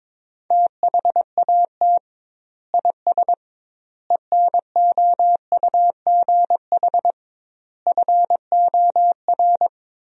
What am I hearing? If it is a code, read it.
Morse code, heard as a keyed tone, about 22 wpm, 700 Hz. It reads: THAT IS ENOUGH FOR
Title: 2x Word-Spacing - Sentences from Top 400 Words - 22wpm